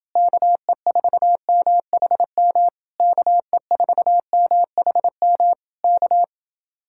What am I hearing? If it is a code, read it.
Morse code, heard as a keyed tone, about 30 wpm, 700 Hz. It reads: KE4MHM KE4MHM K